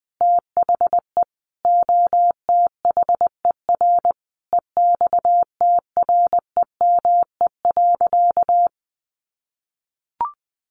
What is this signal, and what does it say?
Morse code, heard as a keyed tone, about 20 wpm, 700 Hz. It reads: THE OTHER EXTREME.
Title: Sentences from Top 1300 Words 20wpm